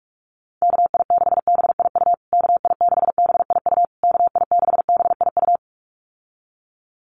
Morse code, heard as a keyed tone, about 44 wpm, 700 Hz. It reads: KI6BIU KI6BIU KI6BIU
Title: USA Call Signs Repeated 3x 45wpm